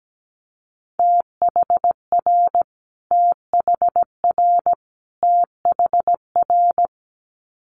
Morse code, heard as a keyed tone, about 17 wpm, 700 Hz. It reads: THR THR THR
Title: Multi-letter Phonemes Repeated 3x 17wpm